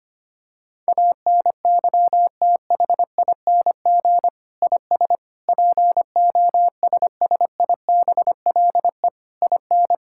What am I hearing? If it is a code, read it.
Morse code, heard as a keyed tone, about 25 wpm, 700 Hz. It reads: ANYTHING IS POSSIBLE IN
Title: Sentences from Top 1000 Words - Review Part 6 - 25wpm